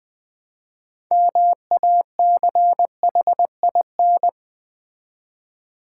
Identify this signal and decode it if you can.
Morse code, heard as a keyed tone, about 20 wpm, 700 Hz. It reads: MACHIN
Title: Build A Word - 3 Letters At A Time - 9-Letter Words - 20wpm